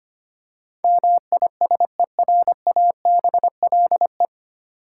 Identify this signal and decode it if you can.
Morse code, heard as a keyed tone, about 25 wpm, 700 Hz. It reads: MISERABLE
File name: practice / 25wpm / 9 Letter Words - First 3 Letters Given - 25wpm